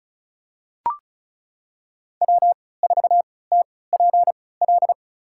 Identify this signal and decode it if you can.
Morse code, heard as a keyed tone, about 34 wpm, spaced as for 20 wpm, 700 Hz. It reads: W4TPL
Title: USA Call Signs - 5 Characters - 3x Character Spacing - 35wpm